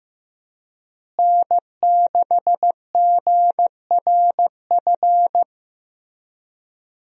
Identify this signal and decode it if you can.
Morse code, heard as a keyed tone, about 16 wpm, 700 Hz. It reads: N6GRF
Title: USA Call Signs - 5 Characters - Repeated 3x - 15wpm